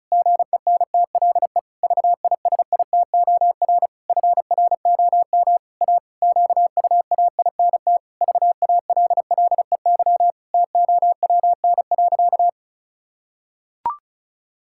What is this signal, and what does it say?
Morse code, heard as a keyed tone, about 34 wpm, 700 Hz. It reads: GENTLE VISITOR FROM A QUAINT VALLEY TOWN.
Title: Pangram Challenge v2 - 35wpm